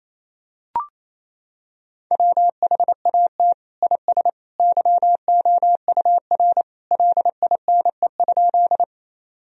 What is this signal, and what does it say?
Morse code, heard as a keyed tone, about 28 wpm, 700 Hz. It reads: WHAT IS YOUR LINE?
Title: Sentences from Top 1000 Words - Review Part 1 - 28wpm